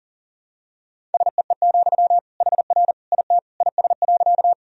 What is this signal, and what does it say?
Morse code, heard as a keyed tone, about 40 wpm, 700 Hz. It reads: SEE, HR IT IS.
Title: Sentences from Top 200 Words and Abbreviations 40wpm